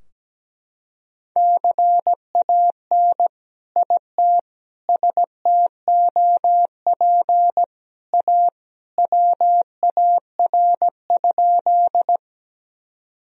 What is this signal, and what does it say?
Morse code, heard as a keyed tone, about 17 wpm, 700 Hz. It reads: CAN IT STOP A WAR?
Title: Sentences from Top 1000 Words - Review Part 2 - 17wpm